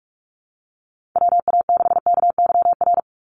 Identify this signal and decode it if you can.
Morse code, heard as a keyed tone, about 44 wpm, 700 Hz. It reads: WA6KYR
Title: NA SOTA Chaser Callsigns 45wpm